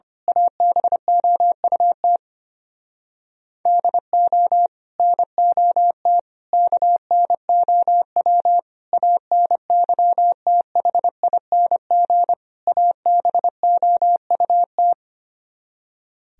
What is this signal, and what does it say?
Morse code, heard as a keyed tone, about 25 wpm, 700 Hz. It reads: ABOUT DO NOT KNOW ANYTHING ABOUT
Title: Speed-Racing - Sets of 5 Words 25wpm